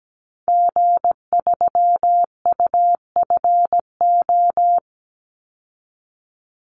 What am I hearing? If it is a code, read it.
Morse code, heard as a keyed tone, about 17 wpm, 700 Hz. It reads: G3UFO
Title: Call Signs Repeated 3x - 17wpm